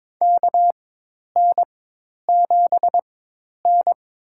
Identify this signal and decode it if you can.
Morse code, heard as a keyed tone, about 23 wpm, spaced as for 10 wpm, 700 Hz. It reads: KN7N
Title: USA Call Signs - 4 Characters - 4x Character Spacing - 22wpm